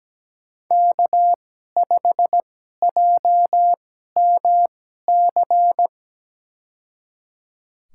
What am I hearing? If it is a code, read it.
Morse code, heard as a keyed tone, about 17 wpm, spaced as for 12 wpm, 700 Hz. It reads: K5JMC
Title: USA Call Signs - 5 Characters - 2x Character Spacing - 17wpm